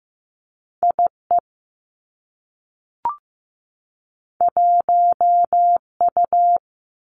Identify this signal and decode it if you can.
Morse code, heard as a keyed tone, about 15 wpm, 700 Hz. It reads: I E 1 U
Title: Two Challenging Letter-Number Combinations 15wpm